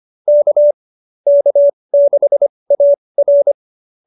Morse code, heard as a keyed tone, about 25 wpm, spaced as for 20 wpm, 590 Hz. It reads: K K6AR
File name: QSO307 20